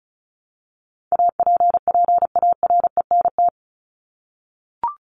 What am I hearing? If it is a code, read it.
Morse code, heard as a keyed tone, about 34 wpm, 700 Hz. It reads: APPARENT
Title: LY Suffix - Top 100 Words with Base Words 35wpm